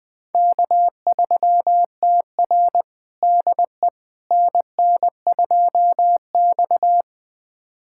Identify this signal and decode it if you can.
Morse code, heard as a keyed tone, about 21 wpm, 700 Hz. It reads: K3TR DE NN2X
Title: USA Callsign DE Callsign - 4 Characters - 20wpm